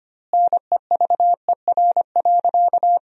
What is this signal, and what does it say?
Morse code, heard as a keyed tone, about 25 wpm, 700 Hz. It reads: NEVER.
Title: 2x Word-Spacing - Sentences from Top 400 Words - 25wpm